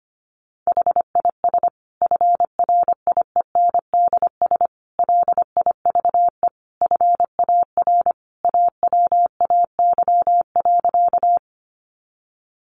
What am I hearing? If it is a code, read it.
Morse code, heard as a keyed tone, about 25 wpm, 700 Hz. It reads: HIS FRIENDS LIVE FAR AWAY.
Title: Additional Easy Phrases 25wpm